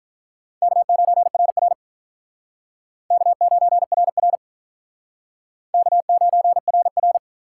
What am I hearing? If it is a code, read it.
Morse code, heard as a keyed tone, about 48 wpm, 700 Hz. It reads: K9RR K9RR K9RR